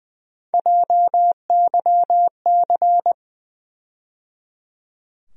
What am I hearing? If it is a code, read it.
Morse code, heard as a keyed tone, about 21 wpm, 700 Hz. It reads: JYC
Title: USA Call Sign Postfixes - 3 Letters 20wpm